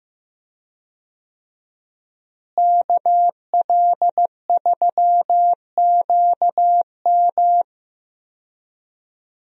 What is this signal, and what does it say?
Morse code, heard as a keyed tone, about 15 wpm, 700 Hz. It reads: KL3QM
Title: USA Call Signs - 5 Characters - Repeated 2x - 15wpm